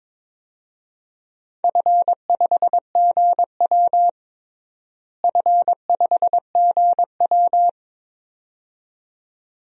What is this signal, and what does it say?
Morse code, heard as a keyed tone, about 21 wpm, 700 Hz. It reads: F5GW F5GW
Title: DXCC-100 Call Signs - 4 Characters - Repeated 2x - 22wpm